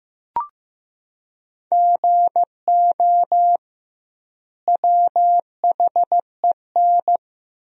Morse code, heard as a keyed tone, about 15 wpm, 700 Hz. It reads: GO WHEN
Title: 2x Word-Spacing - Sentences from Top 500 Words - 15wpm